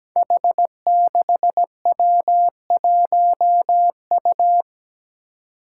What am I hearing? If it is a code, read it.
Morse code, heard as a keyed tone, about 17 wpm, 700 Hz. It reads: H6W1U
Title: Five Challenging Letter-Number Combinations 17wpm